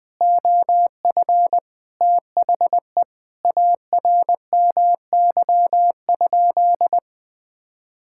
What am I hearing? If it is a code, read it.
Morse code, heard as a keyed tone, about 20 wpm, 700 Hz. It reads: OF THE ARMY?
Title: Sentences from Top 1000 Words - Review Part 6 - 20wpm